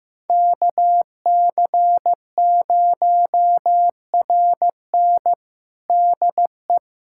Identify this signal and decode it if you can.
Morse code, heard as a keyed tone, about 15 wpm, 700 Hz. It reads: KC0RN DE
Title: USA Callsign DE Callsign 15wpm